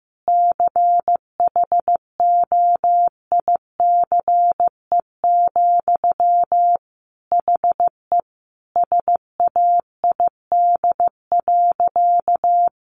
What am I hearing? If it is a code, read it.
Morse code, heard as a keyed tone, about 15 wpm, 700 Hz. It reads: CHOICE, HE SAID.
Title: Sentences from Top 900 Words 15wpm